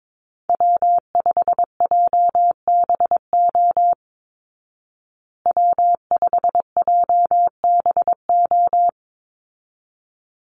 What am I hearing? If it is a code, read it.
Morse code, heard as a keyed tone, about 22 wpm, 700 Hz. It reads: W5JBO W5JBO
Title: Call Signs Repeated 3x - 22wpm